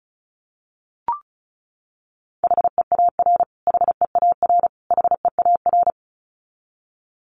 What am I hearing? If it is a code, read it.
Morse code, heard as a keyed tone, about 34 wpm, 700 Hz. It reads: HEAR HEAR HEAR
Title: Top 500 Words Repeated 35wpm